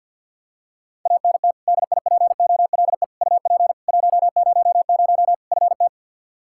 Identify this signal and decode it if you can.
Morse code, heard as a keyed tone, about 51 wpm, 700 Hz. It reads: ANT DIPOLE UP 100 FT
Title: Practice QSO - Exchange 2 - Info - 50wpm